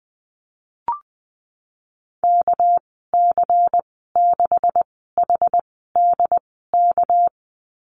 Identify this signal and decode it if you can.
Morse code, heard as a keyed tone, about 20 wpm, spaced as for 15 wpm, 700 Hz. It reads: KC6HDK
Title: 200 Lesson 32g - Letter k - Call Signs 20wpm